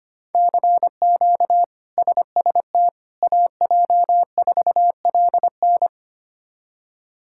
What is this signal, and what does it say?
Morse code, heard as a keyed tone, about 25 wpm, 700 Hz. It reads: CQ SST AJ4LN
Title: K1USN SST QSO Practice - 25wpm